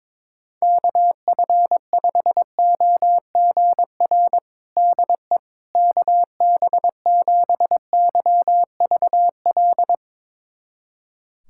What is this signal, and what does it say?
Morse code, heard as a keyed tone, about 22 wpm, 700 Hz. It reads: KF5OGR DE KB7YVL
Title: USA Callsign DE Callsign - 6 Characters - 22wpm